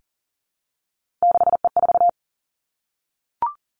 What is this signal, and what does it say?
Morse code, heard as a keyed tone, about 42 wpm, 700 Hz. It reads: 6E4
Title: Three Challenging Letter-Number Combinations 40wpm